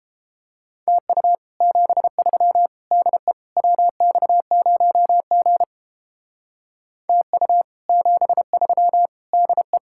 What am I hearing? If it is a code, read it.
Morse code, heard as a keyed tone, about 29 wpm, 700 Hz. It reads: TU 73 DE WX0G TU 73 DE